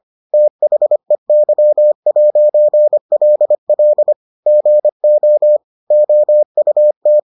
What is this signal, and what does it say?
Morse code, heard as a keyed tone, about 25 wpm, 600 Hz. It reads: THEY'LL GO OUT